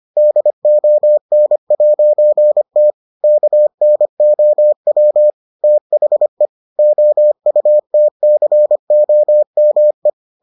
Morse code, heard as a keyed tone, about 25 wpm, 600 Hz. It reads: DON'T KNOW THE OUTCOME